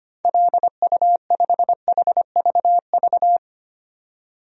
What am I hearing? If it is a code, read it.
Morse code, heard as a keyed tone, about 25 wpm, 700 Hz. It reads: LU5HVV
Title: DXCC-100 Call Signs - 6 Characters - 25wpm